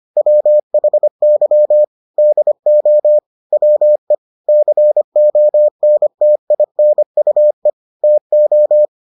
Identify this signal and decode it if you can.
Morse code, heard as a keyed tone, about 25 wpm, 600 Hz. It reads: WHY DO WE CONTINUE TO